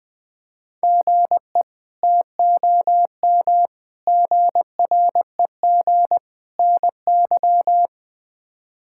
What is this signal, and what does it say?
Morse code, heard as a keyed tone, about 20 wpm, 700 Hz. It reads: GE TOM GREG NY
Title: K1USN SST QSO Practice - 20wpm